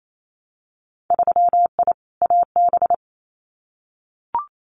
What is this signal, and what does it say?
Morse code, heard as a keyed tone, about 28 wpm, 700 Hz. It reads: 3I AB
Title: Winter Field Day Exchanges Encore 28wpm